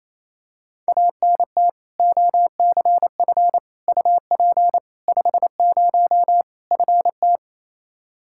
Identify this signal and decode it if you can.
Morse code, heard as a keyed tone, about 27 wpm, 700 Hz. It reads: ANT OCF UP 50 FT
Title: Practice QSO - Exchange 2 - Info - 28wpm